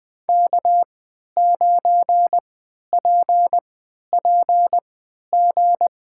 Morse code, heard as a keyed tone, about 20 wpm, spaced as for 11 wpm, 700 Hz. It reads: K9PPG
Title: USA Call Signs - 5 Characters - 3x Character Spacing - 20wpm